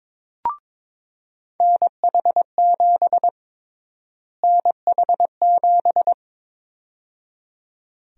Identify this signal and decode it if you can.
Morse code, heard as a keyed tone, about 21 wpm, 700 Hz. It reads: NH7 NH7
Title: USA Call Sign Prefixes - Repeated 2x - 22wpm